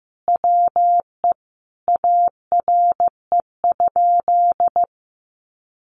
Morse code, heard as a keyed tone, about 15 wpm, 700 Hz. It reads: WE ARE?
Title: A Sound Of Thunder 15wpm